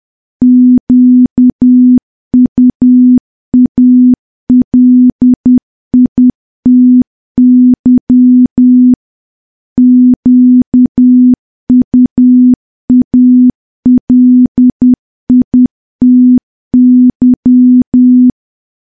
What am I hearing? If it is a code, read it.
Morse code, heard as a keyed tone, about 10 wpm, 250 Hz. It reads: QUALITY QUALITY
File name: morse / morse-training-10WPM